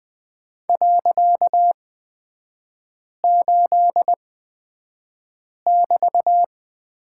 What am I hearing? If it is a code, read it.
Morse code, heard as a keyed tone, about 20 wpm, spaced as for 5 wpm, 700 Hz. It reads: .8<BT>
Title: 255 Lesson 41e - Period - Three Characters 20wpm